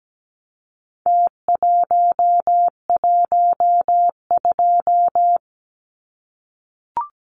Morse code, heard as a keyed tone, about 17 wpm, 700 Hz. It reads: T112
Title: Numbers - Contest Style - 4-Digit 17wpm